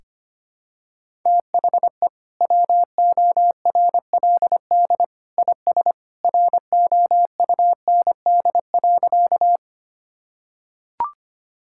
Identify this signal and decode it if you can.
Morse code, heard as a keyed tone, about 25 wpm, 700 Hz. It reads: THE WORLD IS ROUND.
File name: practice / 25wpm / Sentences with 2-Word Phrases - 25wpm